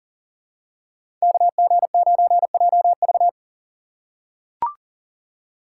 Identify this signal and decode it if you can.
Morse code, heard as a keyed tone, about 40 wpm, 700 Hz. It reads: KG9JV
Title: USA Call Signs - 5 Characters - 2x Character Spacing - 40wpm